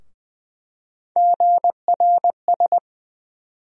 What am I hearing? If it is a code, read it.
Morse code, heard as a keyed tone, about 20 wpm, 700 Hz. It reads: GRS
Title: USA Call Sign Postfixes - 3 Letters 20wpm